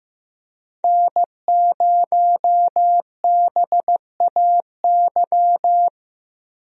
Morse code, heard as a keyed tone, about 15 wpm, 700 Hz. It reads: N0BAY
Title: USA Call Signs - 5 Characters - 15wpm